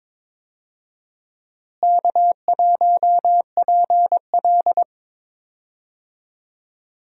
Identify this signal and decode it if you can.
Morse code, heard as a keyed tone, about 22 wpm, 700 Hz. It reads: K1PL
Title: USA Call Signs - 4 Characters - Repeated 2x - 22wpm